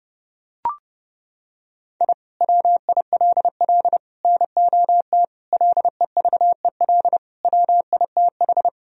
Morse code, heard as a keyed tone, about 29 wpm, 700 Hz. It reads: I WILL NOT LEVEL WITH
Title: Sentences from Top 1000 Words - Review Part 4 - 30wpm